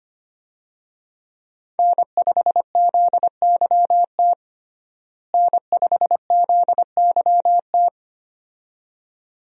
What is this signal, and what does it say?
Morse code, heard as a keyed tone, about 25 wpm, 700 Hz. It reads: N5ZYT N5ZYT
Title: USA Call Signs - 5 Characters - Repeated 2x - 25wpm